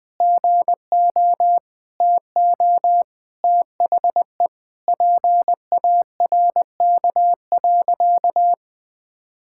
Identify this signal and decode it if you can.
Morse code, heard as a keyed tone, about 20 wpm, 700 Hz. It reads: GO TO THE PARK.
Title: Sentences with 3-Word Phrases - 20wpm